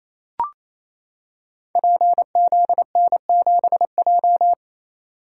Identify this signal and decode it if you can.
Morse code, heard as a keyed tone, about 28 wpm, 690 Hz. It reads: PZN7J
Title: Code Groups - 5 Letter:Numbers - 28wpm